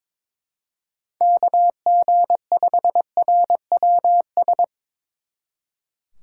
Morse code, heard as a keyed tone, about 24 wpm, 700 Hz. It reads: KG5RWS
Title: Speed Racing - USA 2x3 Call Signs 22wpm